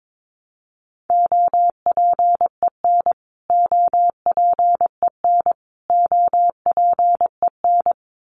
Morse code, heard as a keyed tone, about 22 wpm, 700 Hz. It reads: OPEN OPEN OPEN
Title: Top 500 Words Repeated 22wpm